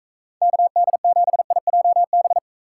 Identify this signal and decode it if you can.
Morse code, heard as a keyed tone, about 45 wpm, 700 Hz. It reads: KD7IJB